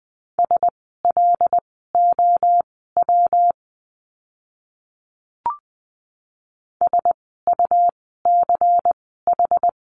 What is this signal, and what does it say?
Morse code, heard as a keyed tone, about 20 wpm, spaced as for 15 wpm, 700 Hz. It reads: SLOW SUCH
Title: 115 Lesson 19f - Letter w - Words 20wpm